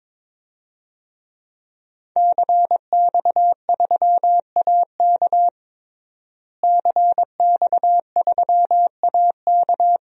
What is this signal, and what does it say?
Morse code, heard as a keyed tone, about 22 wpm, 700 Hz. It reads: CX3AK CX3AK
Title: DXCC-100 Call Signs - 5 Characters - Repeated 2x - 22wpm